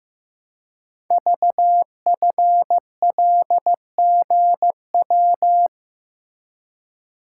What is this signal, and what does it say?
Morse code, heard as a keyed tone, about 15 wpm, 700 Hz. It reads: VFLGW
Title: Code Groups - 5 Letters - 15wpm